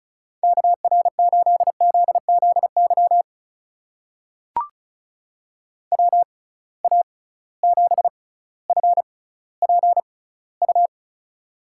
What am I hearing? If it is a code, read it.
Morse code, heard as a keyed tone, about 36 wpm, spaced as for 18 wpm, 690 Hz. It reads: KR8ZZY WA7FPU
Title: USA Call Signs - 6 Characters - 6x Character Spacing - 35wpm